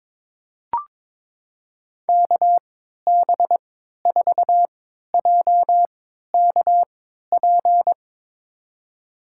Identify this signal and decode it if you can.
Morse code, heard as a keyed tone, about 22 wpm, spaced as for 12 wpm, 700 Hz. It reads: KB4JKP
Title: USA Call Signs - 6 Characters - 3x Character Spacing - 22wpm